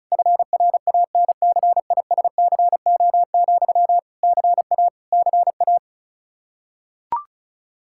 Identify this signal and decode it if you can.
Morse code, heard as a keyed tone, about 34 wpm, 700 Hz. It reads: FRANCISCO, CA CA
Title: K1USN SST QSO Practice 35wpm